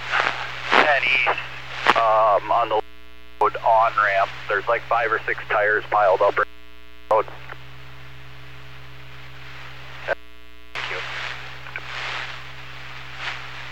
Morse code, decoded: none

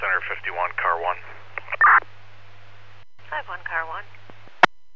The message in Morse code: none

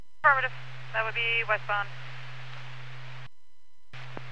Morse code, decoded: none

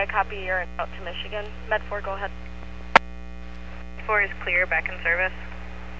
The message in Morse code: none

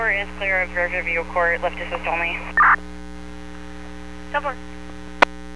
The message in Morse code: none

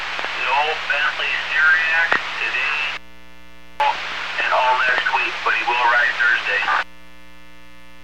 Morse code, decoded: none

